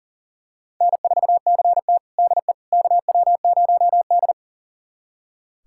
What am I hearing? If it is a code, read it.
Morse code, heard as a keyed tone, about 37 wpm, 690 Hz. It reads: N4CT DE KW0D
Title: USA Callsign DE Callsign - 4 Characters - 40wpm